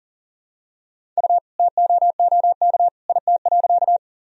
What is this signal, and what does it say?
Morse code, heard as a keyed tone, about 40 wpm, 700 Hz. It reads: U TOOK IT.
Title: Sentences from Top 300 Words and Abbreviations 40wpm